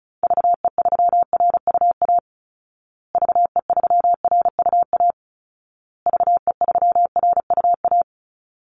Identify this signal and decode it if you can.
Morse code, heard as a keyed tone, about 34 wpm, 700 Hz. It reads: VE3RUA VE3RUA VE3RUA
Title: Call Signs Repeated 2x - 35wpm